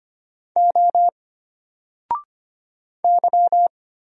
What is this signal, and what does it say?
Morse code, heard as a keyed tone, about 25 wpm, 700 Hz. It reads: O Y
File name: Single-Letter-Number-25wpm